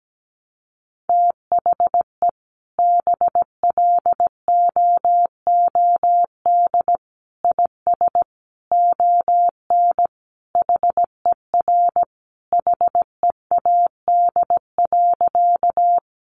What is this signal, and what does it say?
Morse code, heard as a keyed tone, about 17 wpm, 700 Hz. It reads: THE BLOOD IS ON HER HEAD.
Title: Sentences from Top 1100 Words 17wpm